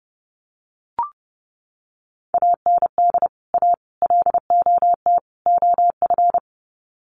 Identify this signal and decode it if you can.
Morse code, heard as a keyed tone, about 30 wpm, 700 Hz. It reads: AND A LOT OF
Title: Sets of 4 Words 30wpm